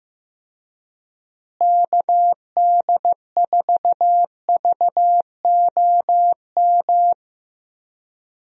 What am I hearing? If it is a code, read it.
Morse code, heard as a keyed tone, about 15 wpm, 700 Hz. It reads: KD4VOM
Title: USA Call Signs - 6 Characters - Repeated 3x - 15wpm